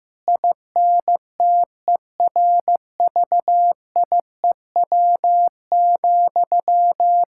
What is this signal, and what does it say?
Morse code, heard as a keyed tone, about 16 wpm, 700 Hz. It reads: INTERVIEW,
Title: Sentences from Top 1600 Words 15wpm